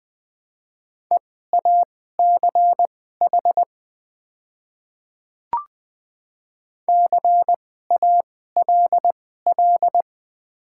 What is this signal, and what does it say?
Morse code, heard as a keyed tone, about 20 wpm, spaced as for 14 wpm, 700 Hz. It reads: EACH CALL
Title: 2x Character Spacing - Top 100 Words - 20wpm